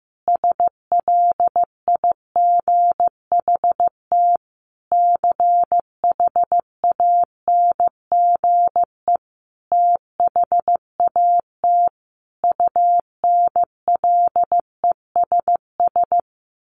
Morse code, heard as a keyed tone, about 15 wpm, 700 Hz. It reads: SLIGHT CHANGE THAT UNLESS
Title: A Sound Of Thunder 15wpm